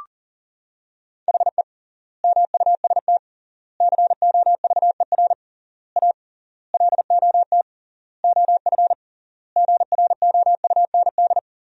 Morse code, heard as a keyed tone, about 40 wpm, 700 Hz. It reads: HE MUST COVER A LOT OF GROUND
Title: 3x Word-Spacing - Sentences from Top 500 Words - 40wpm